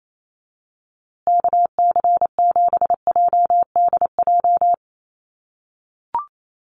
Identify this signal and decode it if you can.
Morse code, heard as a keyed tone, about 28 wpm, 700 Hz. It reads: KC7JDJ
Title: USA Call Signs Repeated 2x 28wpm